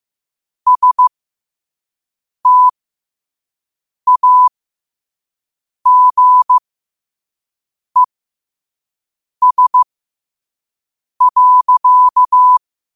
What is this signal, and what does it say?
Morse code, heard as a keyed tone, about 15 wpm, spaced as for 5 wpm, 1000 Hz. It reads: STAGES.
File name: qotdfiles / 5wpm